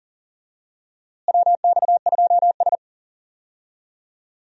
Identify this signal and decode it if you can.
Morse code, heard as a keyed tone, about 39 wpm, 700 Hz. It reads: WX2S
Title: USA Call Signs - 4 Characters - Repeated 3x - 40wpm